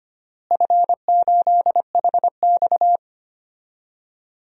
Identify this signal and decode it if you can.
Morse code, heard as a keyed tone, about 25 wpm, 700 Hz. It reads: F8HX
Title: DXCC-100 Call Signs - 4 Characters - Repeated 2x - 25wpm